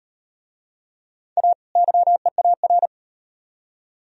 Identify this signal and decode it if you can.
Morse code, heard as a keyed tone, about 36 wpm, 700 Hz. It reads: A YEAR